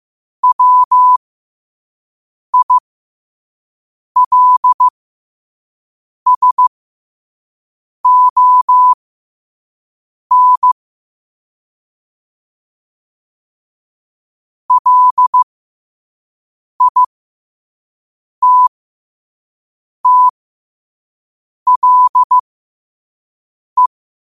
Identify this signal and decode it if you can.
Morse code, heard as a keyed tone, about 15 wpm, spaced as for 5 wpm, 1000 Hz. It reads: WILSON LITTLE
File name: qotdfiles / 5wpm